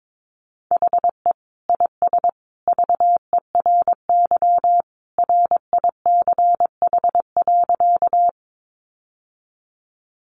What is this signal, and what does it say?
Morse code, heard as a keyed tone, about 22 wpm, 700 Hz. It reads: HE IS VERY RICH.
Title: Sentences from Top 1100 Words 22wpm